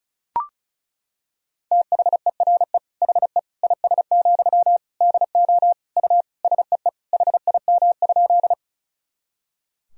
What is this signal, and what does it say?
Morse code, heard as a keyed tone, about 34 wpm, 700 Hz. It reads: THERE HE IS, DO U SEE HIM?
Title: Sentences from Top 100 Words and Abbreviations 35wpm